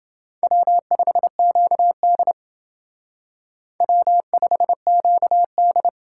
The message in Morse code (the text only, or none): W5QD W5QD